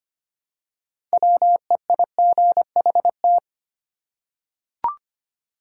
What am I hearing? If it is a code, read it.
Morse code, heard as a keyed tone, about 25 wpm, 700 Hz. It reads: WEIGHT
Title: Six Letter Words - 25wpm